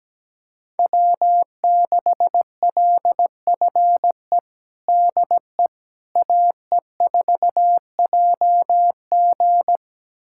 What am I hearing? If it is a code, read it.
Morse code, heard as a keyed tone, about 17 wpm, 700 Hz. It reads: W6LFE DE AE4JG
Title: USA Callsign DE Callsign - 5 Characters - 17wpm